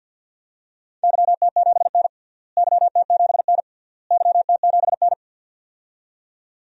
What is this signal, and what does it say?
Morse code, heard as a keyed tone, about 47 wpm, 700 Hz. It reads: YT7N YT7N YT7N
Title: DXCC-100 Call Signs - 4 Characters - Repeated 3x - 50wpm